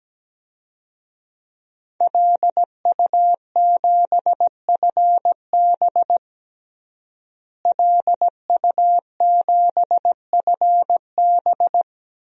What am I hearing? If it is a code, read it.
Morse code, heard as a keyed tone, about 17 wpm, 700 Hz. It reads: LU7FB LU7FB
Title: DXCC-100 Call Signs - 5 Characters - Repeated 3x - 17wpm